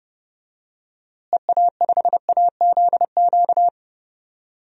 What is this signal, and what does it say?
Morse code, heard as a keyed tone, about 30 wpm, 700 Hz. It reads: EA5AZQ
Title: DXCC-100 Call Signs - 6 Characters - Repeated 3x - 30wpm